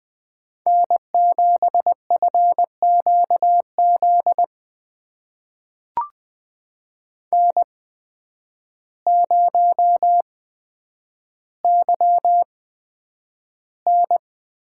Morse code, heard as a keyed tone, about 20 wpm, spaced as for 9 wpm, 700 Hz. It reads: N7FQZ N0YN
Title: USA Call Signs - 8x Character Spacing - 20wpm